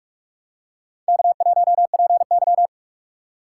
Morse code, heard as a keyed tone, about 43 wpm, 700 Hz. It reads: K1PY
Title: USA Call Signs - 4 Characters - 3x Character Spacing - 45wpm